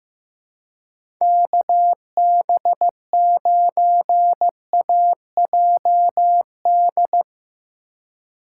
Morse code, heard as a keyed tone, about 15 wpm, 700 Hz. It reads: KB9AJD
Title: USA Call Signs - 4x Character Spacing - 15wpm